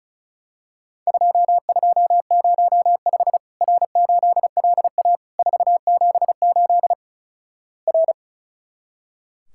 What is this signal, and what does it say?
Morse code, heard as a keyed tone, about 36 wpm, 700 Hz. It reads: 2205 R8LA 478 R
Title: Worked All Europe DX Contest - QTC Exchange - 35wpm